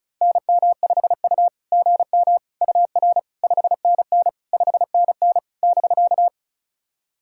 Morse code, heard as a keyed tone, about 34 wpm, 660 Hz. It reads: NM5U GM UR 5NN 5NN <BK>
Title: SOTA QSO Practice 35wpm